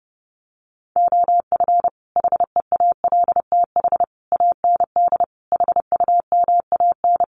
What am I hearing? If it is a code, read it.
Morse code, heard as a keyed tone, about 28 wpm, 700 Hz. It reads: OF HEALTH AND HUMAN
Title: Sets of 5 Words 30wpm